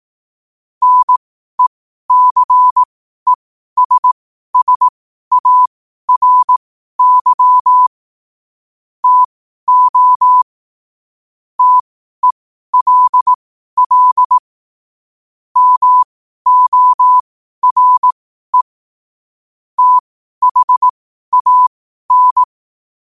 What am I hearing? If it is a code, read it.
Morse code, heard as a keyed tone, about 18 wpm, spaced as for 12 wpm, 1000 Hz. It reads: NECESSARY TO TELL MORE THAN
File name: qotdfiles / 12wpm